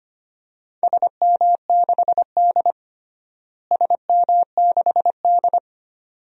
Morse code, heard as a keyed tone, about 25 wpm, 700 Hz. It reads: SM6D SM6D
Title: DXCC-100 Call Signs - 4 Characters - Repeated 2x - 25wpm